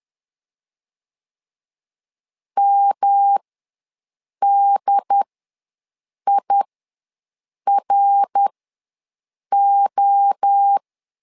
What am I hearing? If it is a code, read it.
Morse code, heard as a keyed tone, about 10 wpm, spaced as for 6 wpm, 800 Hz. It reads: MDIRO
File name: LEZIONI / 3 MWRO / 3 MWRO ws 10wpm 2310